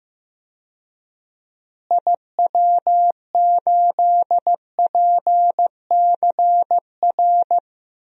Morse code, heard as a keyed tone, about 15 wpm, 700 Hz. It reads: IW8PCR
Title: DXCC-100 Call Signs - 6 Characters - Repeated 3x - 15wpm